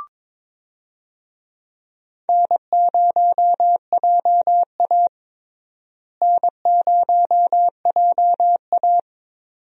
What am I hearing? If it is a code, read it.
Morse code, heard as a keyed tone, about 22 wpm, 700 Hz. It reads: N0JA N0JA
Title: USA Call Signs - 4 Characters - Repeated 2x - 22wpm